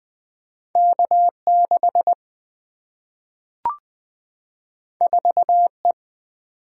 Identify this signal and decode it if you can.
Morse code, heard as a keyed tone, about 20 wpm, 700 Hz. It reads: K6 4E
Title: Single-Letter Single-Number 20wpm